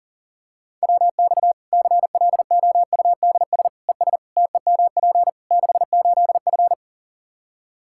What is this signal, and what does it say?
Morse code, heard as a keyed tone, about 40 wpm, 700 Hz. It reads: WX CLOUDS ES TEMP 68F
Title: Practice QSO - Exchange 2 - Info - 40wpm